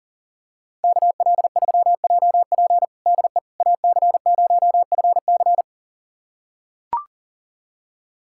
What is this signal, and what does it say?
Morse code, heard as a keyed tone, about 40 wpm, 700 Hz. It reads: KL3JP DE AC0FC
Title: USA Callsign DE Callsign - 5 Characters - 40wpm